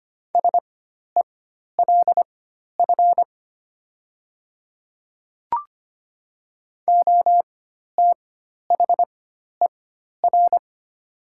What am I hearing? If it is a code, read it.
Morse code, heard as a keyed tone, about 25 wpm, spaced as for 12 wpm, 690 Hz. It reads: SELF OTHER
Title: 4x Character Spacing - Top 200 Words - 25wpm